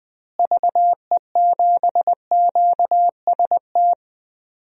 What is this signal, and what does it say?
Morse code, heard as a keyed tone, about 20 wpm, 700 Hz. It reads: VE7QST
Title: DXCC-100 Call Signs - 6 Characters - Repeated 3x - 20wpm